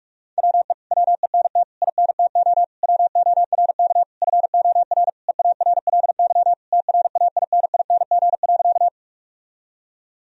Morse code, heard as a keyed tone, about 44 wpm, 700 Hz. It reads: WE WENT INTO WORK FOR EARLY TRAINING.
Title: Sentences from Top 1000 Words - Review Part 4 - 45wpm